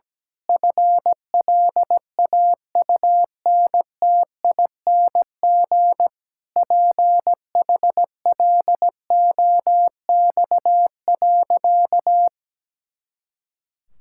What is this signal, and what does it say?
Morse code, heard as a keyed tone, about 17 wpm, 700 Hz. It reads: FLAUNTING PHLOX.
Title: Pangram Challenge v2 - 17wpm